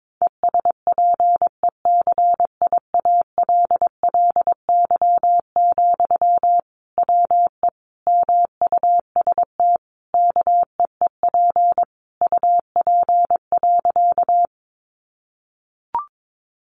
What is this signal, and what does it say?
Morse code, heard as a keyed tone, about 22 wpm, 700 Hz. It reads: ESPECIALLY, WE MUST KEEP UP.
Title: Sentences from Top 700 Words 22wpm